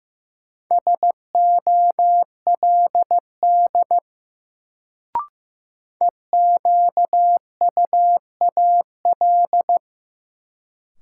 Morse code, heard as a keyed tone, about 15 wpm, 700 Hz. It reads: SOLD EQUAL
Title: Sentences from Top 1400 Words - New Word Drill - 15wpm